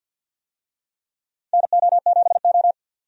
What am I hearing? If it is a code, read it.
Morse code, heard as a keyed tone, about 49 wpm, 700 Hz. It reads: NO7O
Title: USA Call Signs - 4 Characters - 50wpm